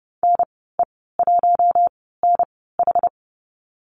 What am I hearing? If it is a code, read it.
Morse code, heard as a keyed tone, about 30 wpm, spaced as for 17 wpm, 700 Hz. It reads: NE1NH
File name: Lesson 14g - Number 2 - Call Signs 30wpm